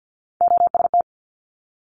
Morse code, heard as a keyed tone, about 50 wpm, 700 Hz. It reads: OST
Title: Multi-letter Phonemes Quick Fire 50wpm